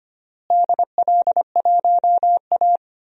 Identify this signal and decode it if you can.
Morse code, heard as a keyed tone, about 25 wpm, 700 Hz. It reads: DL1A
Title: DXCC-100 Call Signs - 4 Characters - Repeated 3x - 25wpm